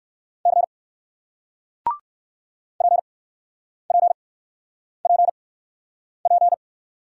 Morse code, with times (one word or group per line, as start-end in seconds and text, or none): 0.5-0.6 C
2.8-6.5 PPPP